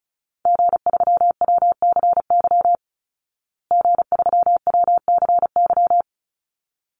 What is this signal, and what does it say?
Morse code, heard as a keyed tone, about 34 wpm, 700 Hz. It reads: G3WCY G3WCY
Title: Call Signs Repeated 3x - 35wpm